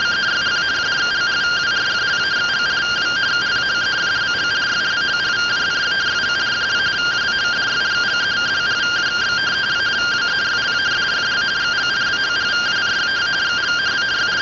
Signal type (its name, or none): two-tone teleprinter signal